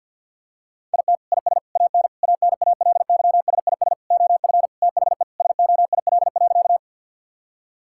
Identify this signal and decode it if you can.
Morse code, heard as a keyed tone, about 50 wpm, 700 Hz. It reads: IT IS AN ANALYSIS OF THE SOIL.
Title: Sentences from Top 1000 Words - Review Part 5 - 50wpm